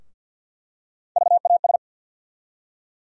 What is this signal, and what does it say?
Morse code, heard as a keyed tone, about 50 wpm, 700 Hz. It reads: VAS
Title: USA Call Sign Postfixes - 3 Letters 50wpm